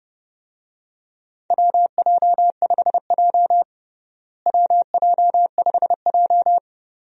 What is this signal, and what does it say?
Morse code, heard as a keyed tone, about 30 wpm, 700 Hz. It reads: WJ5J WJ5J
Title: USA Call Signs - 4 Characters - Repeated 2x - 30wpm